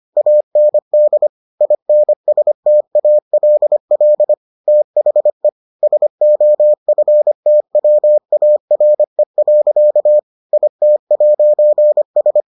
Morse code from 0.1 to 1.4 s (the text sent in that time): AND